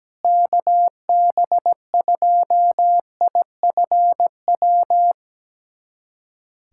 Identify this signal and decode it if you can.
Morse code, heard as a keyed tone, about 17 wpm, 700 Hz. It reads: KB2IFW
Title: USA Callsign DE Callsign 17wpm